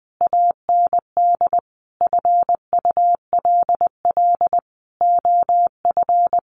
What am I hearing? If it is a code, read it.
Morse code, heard as a keyed tone, about 20 wpm, 700 Hz. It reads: AND FULL OF
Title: Sentences from Top 1500 Words 20wpm